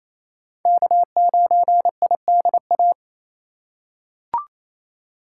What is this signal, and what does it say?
Morse code, heard as a keyed tone, about 28 wpm, 700 Hz. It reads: K9IDA
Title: USA Call Signs - 5 Characters - 2x Character Spacing - 28wpm